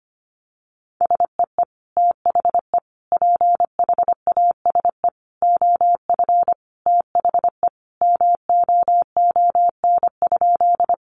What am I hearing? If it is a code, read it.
Morse code, heard as a keyed tone, about 25 wpm, 700 Hz. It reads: SEE THE PHASE OF THE MOON?
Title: Sentences from Top 1200 Words 25wpm